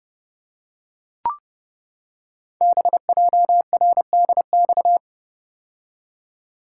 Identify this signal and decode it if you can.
Morse code, heard as a keyed tone, about 30 wpm, 700 Hz. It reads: BJRDX
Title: Code Groups - 5 Letters - 30wpm